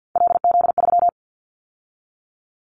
Morse code, heard as a keyed tone, about 50 wpm, 700 Hz. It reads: LZ3
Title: Callsign Prefixes for Top 50 DXCC - 50wpm